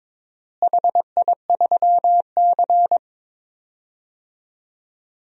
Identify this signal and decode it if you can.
Morse code, heard as a keyed tone, about 22 wpm, 700 Hz. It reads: HI3C
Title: DXCC-100 Call Signs - 4 Characters - Repeated 2x - 22wpm